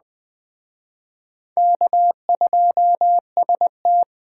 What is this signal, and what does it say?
Morse code, heard as a keyed tone, about 20 wpm, 700 Hz. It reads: K2ST
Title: Speed Racing - USA Call Signs 17wpm